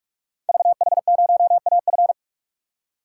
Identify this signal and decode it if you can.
Morse code, heard as a keyed tone, about 44 wpm, 690 Hz. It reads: VH0AF
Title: Code Groups - 5 Letter:Numbers - 45wpm